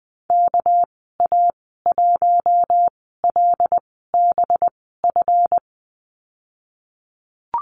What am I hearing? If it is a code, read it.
Morse code, heard as a keyed tone, about 20 wpm, spaced as for 15 wpm, 700 Hz. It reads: KA1LBF
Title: 207 Lesson 33g - Letter j - Call Signs 20wpm